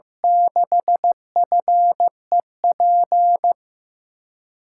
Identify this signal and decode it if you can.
Morse code, heard as a keyed tone, about 15 wpm, 700 Hz. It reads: 6FEP
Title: USA Call Signs - 6 Characters - 6x Character Spacing - 15wpm